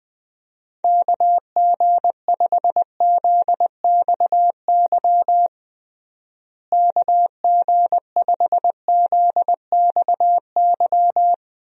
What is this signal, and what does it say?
Morse code, heard as a keyed tone, about 20 wpm, 700 Hz. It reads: KG5ZXY KG5ZXY
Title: USA Call Signs - 6 Characters - Repeated 2x - 20wpm